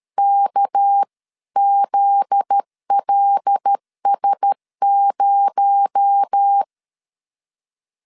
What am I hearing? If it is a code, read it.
Morse code, heard as a keyed tone, about 13 wpm, 800 Hz. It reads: KZLS0
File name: CW / DET / 1 9 1500C S C 12wpm 3811